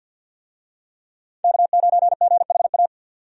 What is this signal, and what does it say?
Morse code, heard as a keyed tone, about 52 wpm, 700 Hz. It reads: K9GHA
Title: USA Call Signs - 5 Characters - 50wpm